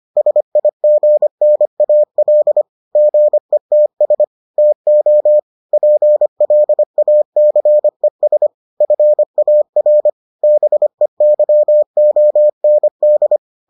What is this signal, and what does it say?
Morse code, heard as a keyed tone, about 25 wpm, 600 Hz. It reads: SIGNAL GETS TO PLACES FAR BEYOND